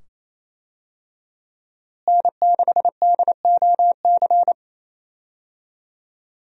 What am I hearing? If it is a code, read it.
Morse code, heard as a keyed tone, about 28 wpm, 700 Hz. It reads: N6DOC
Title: USA Call Signs - 5 Characters - Repeated 2x - 28wpm